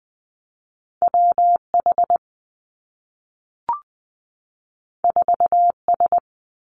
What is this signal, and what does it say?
Morse code, heard as a keyed tone, about 20 wpm, 700 Hz. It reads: W H 4 S
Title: Two Challenging Letter-Number Combinations 20wpm